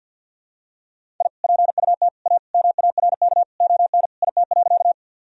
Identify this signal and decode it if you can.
Morse code, heard as a keyed tone, about 47 wpm, 700 Hz. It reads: I PUT A MARK ON IT.
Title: Sentences from Top 1000 Words - Review Part 1 - 50wpm